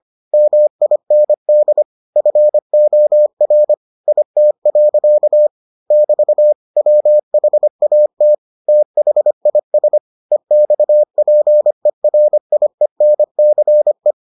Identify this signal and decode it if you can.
Morse code, heard as a keyed tone, about 25 wpm, 600 Hz. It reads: MIND FOR IT. <BT> WHAT THIS EXPERIENCE